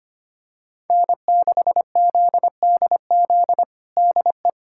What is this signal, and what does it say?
Morse code, heard as a keyed tone, about 25 wpm, 700 Hz. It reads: N6ZDZ DE
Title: USA Callsign DE Callsign - 5 Characters - 25wpm